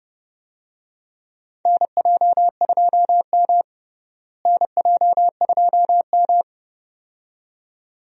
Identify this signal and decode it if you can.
Morse code, heard as a keyed tone, about 30 wpm, 700 Hz. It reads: NJ2M NJ2M
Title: USA Call Signs - 4 Characters - Repeated 2x - 30wpm